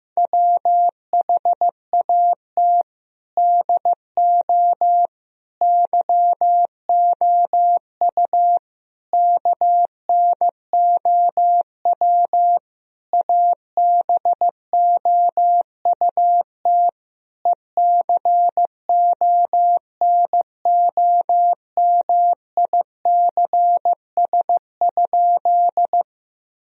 Morse code, heard as a keyed tone, about 15 wpm, 700 Hz. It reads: WHAT DO YOU KNOW ABOUT ECONOMICS?
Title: Sentences from Top 1000 Words - Review Part 3 - 15wpm